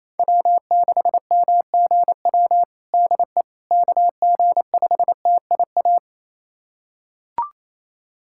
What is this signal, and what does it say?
Morse code, heard as a keyed tone, about 28 wpm, 700 Hz. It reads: W6MGW DE KG5TIA
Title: USA Callsign DE Callsign 28wpm